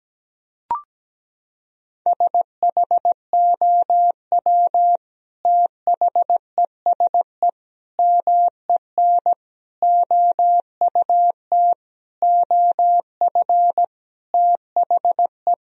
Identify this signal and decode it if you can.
Morse code, heard as a keyed tone, about 17 wpm, 700 Hz. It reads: SHOW THESE MEN OUT OF THE
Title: Sentences from Top 1000 Words - Review Part 1 - 17wpm